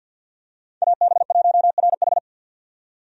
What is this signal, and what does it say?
Morse code, heard as a keyed tone, about 50 wpm, 700 Hz. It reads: AB1RH
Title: USA Call Signs - 5 Characters - 50wpm